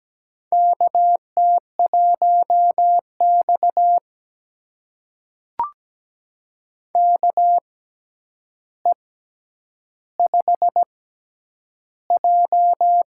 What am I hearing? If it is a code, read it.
Morse code, heard as a keyed tone, about 17 wpm, spaced as for 9 wpm, 700 Hz. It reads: KT1X KE5J
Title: USA Call Signs - 4 Characters - 6x Character Spacing - 17wpm